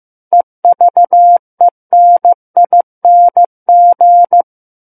Morse code, heard as a keyed tone, about 15 wpm, 700 Hz. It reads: EVENING
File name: Book One / 06-Chapter 6 - 15 WPM